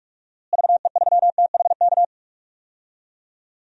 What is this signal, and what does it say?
Morse code, heard as a keyed tone, about 41 wpm, 700 Hz. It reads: VE3THK
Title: DXCC-100 Call Signs - 6 Characters - Repeated 3x - 45wpm